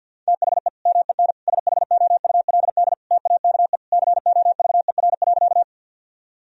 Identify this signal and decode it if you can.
Morse code, heard as a keyed tone, about 50 wpm, 700 Hz. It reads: THE MEN SHOULD TAKE COVER.
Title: Sentences from Top 1000 Words - Review Part 1 - 50wpm